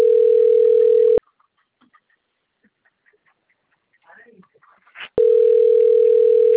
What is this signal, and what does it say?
Ringback tone (North America), 440 + 480 Hz, with speech.
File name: rg-1600-unknown- - - .37582